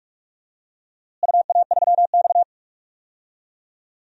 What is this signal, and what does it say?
Morse code, heard as a keyed tone, about 41 wpm, 700 Hz. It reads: UA3X
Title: DXCC-100 Call Signs - 4 Characters - Repeated 2x - 45wpm